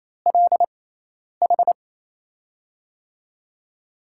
Morse code, heard as a keyed tone, about 28 wpm, spaced as for 10 wpm, 700 Hz. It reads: LH
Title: USA Call Signs - 6 Characters - 6x Character Spacing - 28wpm